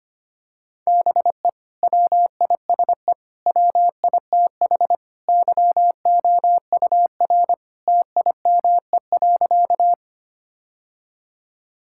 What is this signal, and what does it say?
Morse code, heard as a keyed tone, about 25 wpm, 700 Hz. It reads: BE WISE WITH YOUR TIME.
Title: Sentences from Top 1600 Words 25wpm